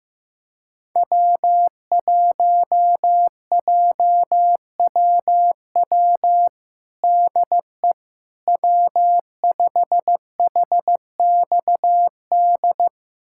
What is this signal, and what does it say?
Morse code, heard as a keyed tone, about 15 wpm, 700 Hz. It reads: W1JWW DE W5HXD
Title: USA Callsign DE Callsign 15wpm